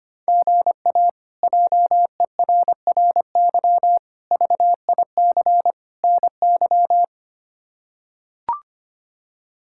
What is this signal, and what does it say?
Morse code, heard as a keyed tone, about 25 wpm, 700 Hz. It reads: GA JERRY VIC NY
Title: K1USN SST QSO Practice - 25wpm